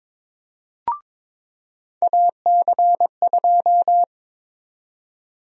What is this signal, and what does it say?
Morse code, heard as a keyed tone, about 22 wpm, 700 Hz. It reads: AC2
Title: USA Call Sign Prefixes - 22wpm